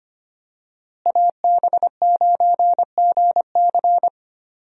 Morse code, heard as a keyed tone, about 25 wpm, 700 Hz. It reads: AB9GC
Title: USA Call Signs - 5 Characters - Repeated 2x - 25wpm